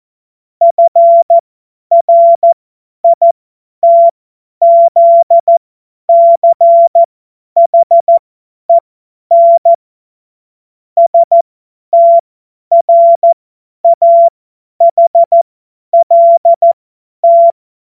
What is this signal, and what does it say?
Morse code, heard as a keyed tone, about 14 wpm, spaced as for 10 wpm, 680 Hz. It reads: FRITZCHEN STRAHLT
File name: Morsekurs / Witze / Witz 13. September 2022 50 F70